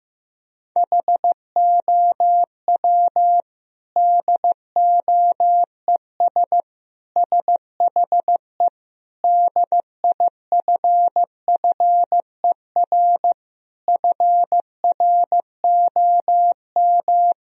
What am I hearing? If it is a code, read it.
Morse code, heard as a keyed tone, about 15 wpm, 700 Hz. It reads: HOW DOES SHE DIFFER FROM
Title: Sentences from Top 1000 Words - Review Part 1 - 15wpm